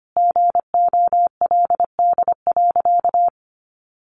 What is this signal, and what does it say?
Morse code, heard as a keyed tone, about 25 wpm, 700 Hz. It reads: GOLD.
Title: Sentences from Top 1500 Words 25wpm